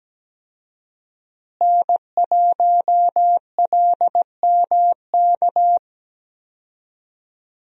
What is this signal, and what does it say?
Morse code, heard as a keyed tone, about 17 wpm, 700 Hz. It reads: N1LMK
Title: USA Call Signs - 5 Characters - Repeated 2x - 17wpm